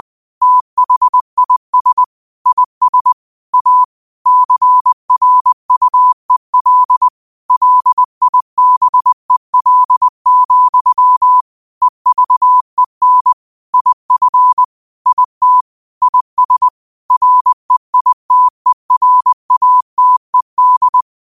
Morse code, heard as a keyed tone, about 20 wpm, 1000 Hz. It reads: THIS IS A CRUEL LIBEL, EVEN IF IT IS REITERATED
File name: qotdfiles / 20wpm